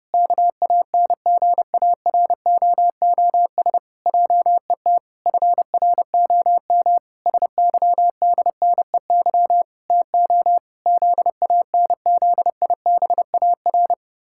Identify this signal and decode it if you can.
Morse code, heard as a keyed tone, about 30 wpm, 700 Hz. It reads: KANGAROOS JET FROM SYDNEY TO ZANZIBAR
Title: Pangram Challenge v2 - 30wpm